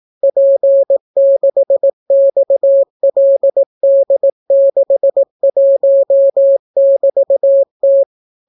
Morse code, heard as a keyed tone, about 18 wpm, 550 Hz. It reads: P6XLD61<BT>T